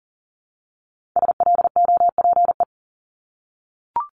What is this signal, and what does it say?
Morse code, heard as a keyed tone, about 40 wpm, 700 Hz. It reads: SLOPE
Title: Multi-letter Phonemes Repeated 3x 40wpm